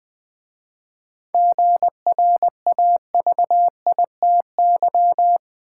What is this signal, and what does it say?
Morse code, heard as a keyed tone, about 20 wpm, 700 Hz. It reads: GRAVITY
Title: 7 Letter Words - First 3 Letters Given - 20wpm